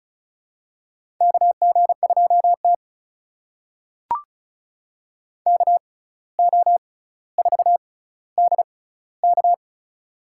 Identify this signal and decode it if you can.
Morse code, heard as a keyed tone, about 36 wpm, spaced as for 17 wpm, 700 Hz. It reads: KG2T KO4DK
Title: USA Call Signs - 6x Character Spacing - 35wpm